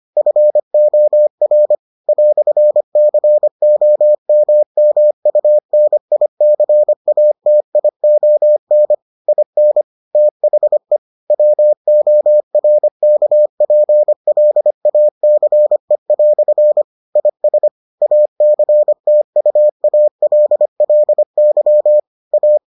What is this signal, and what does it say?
Morse code, heard as a keyed tone, about 25 wpm, 600 Hz. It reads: FOR "COMMUNICATION IN THE WORKPLACE" IS ACTUALLY A